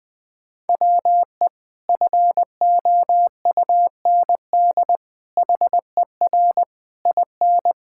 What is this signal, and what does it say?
Morse code, heard as a keyed tone, about 20 wpm, 700 Hz. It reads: WE FOUND HER IN
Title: Sentences with 4-Word Phrases - Spoken Beginning - 20wpm